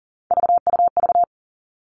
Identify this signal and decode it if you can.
Morse code, heard as a keyed tone, about 38 wpm, 700 Hz. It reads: VUV
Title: Three Challenging Letter-Number Combinations 40wpm